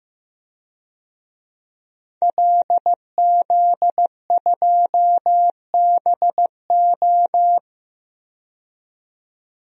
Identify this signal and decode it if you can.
Morse code, heard as a keyed tone, about 15 wpm, 700 Hz. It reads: LZ2BO
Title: DXCC-100 Call Signs - 5 Characters - 15wpm